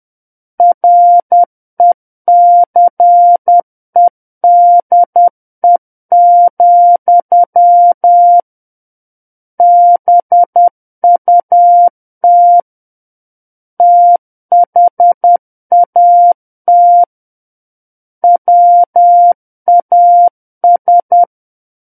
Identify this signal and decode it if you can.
Morse code, heard as a keyed tone, about 10 wpm, 700 Hz. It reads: RECEDE, BUT THAT WAS
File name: Book One / 01-Chapter 1 - 10 WPM